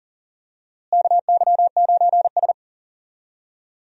Morse code, heard as a keyed tone, about 37 wpm, 700 Hz. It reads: KY9S
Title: USA Call Signs - 4 Characters - Repeated 2x - 40wpm